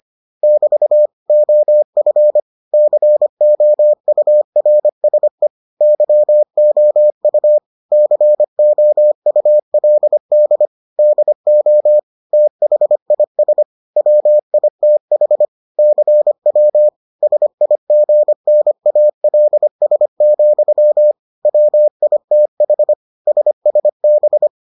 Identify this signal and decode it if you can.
Morse code, heard as a keyed tone, about 25 wpm, 600 Hz. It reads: <BT> OF COURSE YOU COULD DO THIS WITH CW SIGNALS, WITH SSB